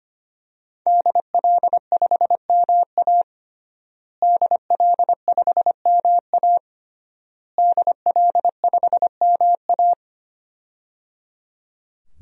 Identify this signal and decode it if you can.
Morse code, heard as a keyed tone, about 25 wpm, 700 Hz. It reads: DL5MA DL5MA DL5MA
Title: DXCC-100 Call Signs - 5 Characters - Repeated 3x - 25wpm